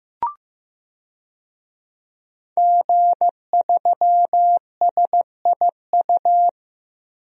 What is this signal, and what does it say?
Morse code, heard as a keyed tone, about 15 wpm, 700 Hz. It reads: G3SIU
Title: DXCC-100 Call Signs - 5 Characters - 15wpm